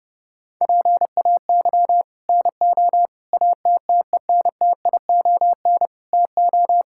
Morse code, heard as a keyed tone, about 30 wpm, 700 Hz. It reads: PAY NO ATTENTION TO
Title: Sentences from Top 1000 Words - Review Part 6 - 30wpm